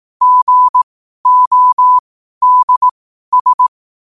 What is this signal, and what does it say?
Morse code, heard as a keyed tone, about 18 wpm, spaced as for 12 wpm, 1000 Hz. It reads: GODS
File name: qotdfiles / 12wpm